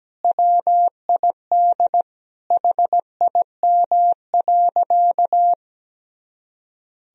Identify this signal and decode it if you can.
Morse code, heard as a keyed tone, about 17 wpm, 700 Hz. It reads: WID HIM.
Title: Sentences from Top 400 Words and Abbreviations 17wpm